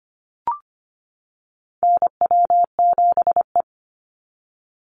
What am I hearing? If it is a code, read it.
Morse code, heard as a keyed tone, about 25 wpm, 700 Hz. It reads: NW7E
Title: NA SOTA Chaser Callsigns 25wpm